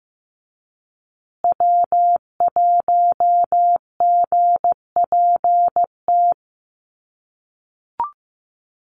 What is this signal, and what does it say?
Morse code, heard as a keyed tone, about 15 wpm, 700 Hz. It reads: W1GPT
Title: USA Call Signs Repeated 3x 15wpm